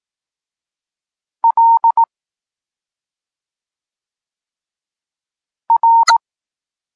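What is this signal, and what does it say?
Morse code, heard as a keyed tone, about 18 wpm, 920 Hz. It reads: L R
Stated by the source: Polish Morse lesson teaching the letter C, from SP9MOA Niepołomicki Klub Krótkofalowców.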